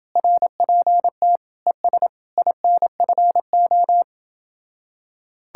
Morse code, heard as a keyed tone, about 29 wpm, 700 Hz. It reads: RPT ES INFO